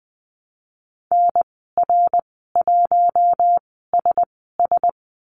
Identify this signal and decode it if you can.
Morse code, heard as a keyed tone, about 20 wpm, spaced as for 15 wpm, 700 Hz. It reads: NR1SS
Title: 055 Lesson 10g - Letter r - Call Signs 20wpm